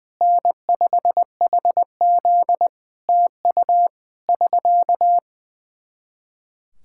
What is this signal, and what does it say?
Morse code, heard as a keyed tone, about 20 wpm, 700 Hz. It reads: N5HZ TU <SK>
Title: Practice QSO - Exchange 4 - Ending - 20wpm